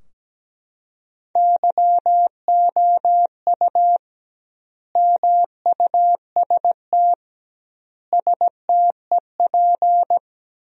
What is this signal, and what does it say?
Morse code, heard as a keyed tone, about 17 wpm, 700 Hz. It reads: YOU MUST STEP
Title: 2x Word-Spacing - Sentences from Top 400 Words - 17wpm